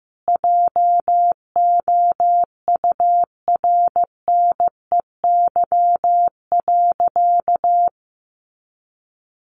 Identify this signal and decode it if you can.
Morse code, heard as a keyed tone, about 15 wpm, 700 Hz. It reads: JOURNEY.
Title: A Sound Of Thunder 15wpm